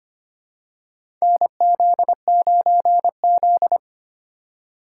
Morse code, heard as a keyed tone, about 25 wpm, 700 Hz. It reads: NZ9Z
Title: USA Call Signs - 4 Characters - 4x Character Spacing - 25wpm